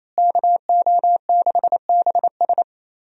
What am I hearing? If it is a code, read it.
Morse code, heard as a keyed tone, about 28 wpm, 700 Hz. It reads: KO6BS
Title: USA Call Signs - 5 Characters - Repeated 3x - 28wpm